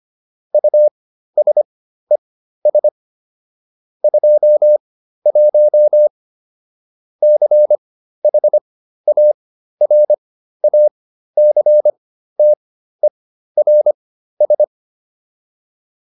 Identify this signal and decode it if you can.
Morse code, heard as a keyed tone, about 25 wpm, spaced as for 13 wpm, 600 Hz. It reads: USES 21 CHARACTERS